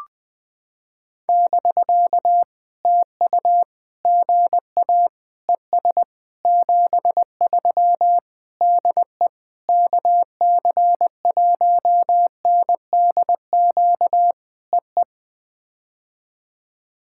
Morse code, heard as a keyed tone, about 21 wpm, 700 Hz. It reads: <BK> TU GA ES 73 DE KC1NDQ EE
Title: POTA QSO Practice - 20wpm